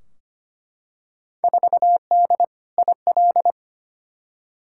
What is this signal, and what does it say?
Morse code, heard as a keyed tone, about 25 wpm, 700 Hz. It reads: 4D IL
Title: Field Day Exchanges Encore - 25wpm